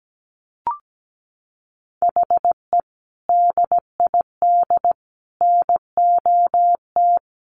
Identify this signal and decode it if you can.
Morse code, heard as a keyed tone, about 17 wpm, 700 Hz. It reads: HE DID NOT
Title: Sentences from Top 1200 Words 17wpm